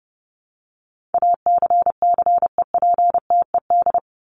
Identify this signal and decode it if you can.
Morse code, heard as a keyed tone, about 30 wpm, 700 Hz. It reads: ACCEPTED
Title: ED Suffix - Top 100 Words with Base Words 30wpm